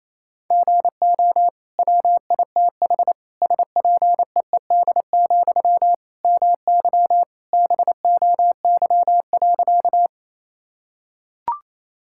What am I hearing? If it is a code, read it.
Morse code, heard as a keyed tone, about 27 wpm, 700 Hz. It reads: GO WITH SPEED, MY BOY.
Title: Sentences from Top 1000 Words - Review Part 5 - 28wpm